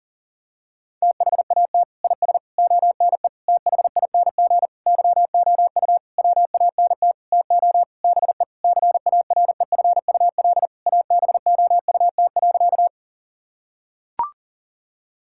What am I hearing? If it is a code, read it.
Morse code, heard as a keyed tone, about 40 wpm, 700 Hz. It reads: THAT IS ONE THING YOU WANT TO BE CAREFUL ABOUT.
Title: Sentences from Top 1600 Words 40wpm